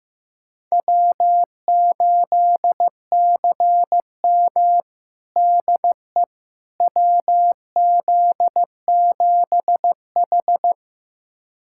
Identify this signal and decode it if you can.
Morse code, heard as a keyed tone, about 15 wpm, 700 Hz. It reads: W8CM DE WZ7H
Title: USA Callsign DE Callsign - 4 Characters - 15wpm